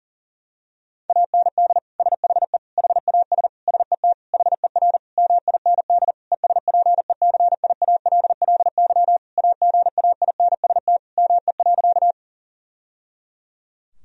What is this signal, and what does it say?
Morse code, heard as a keyed tone, about 40 wpm, 700 Hz. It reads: AND SHE HAS SET HER MIND ESPECIALLY AGAINST ME.
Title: Sentences from Top 1000 Words - Review Part 4 - 40wpm